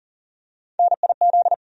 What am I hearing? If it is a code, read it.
Morse code, heard as a keyed tone, about 40 wpm, 700 Hz. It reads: NIZ
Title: USA Call Sign Postfixes - 3 Letters 40wpm